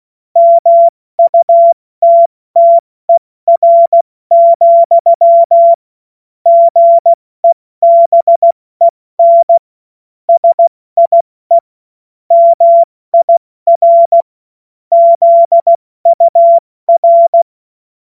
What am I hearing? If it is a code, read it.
Morse code, heard as a keyed tone, about 16 wpm, 680 Hz. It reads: MUTTER, GEBEN SIE MIR ZUR
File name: Morsekurs / Witze / Witz 05 März 2024 70 F80